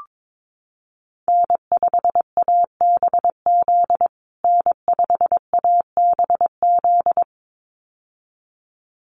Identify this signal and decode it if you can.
Morse code, heard as a keyed tone, about 22 wpm, 700 Hz. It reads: N5ABZ N5ABZ
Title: USA Call Signs Repeated 2x 22wpm